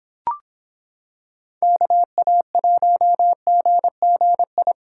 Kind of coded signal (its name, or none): Morse code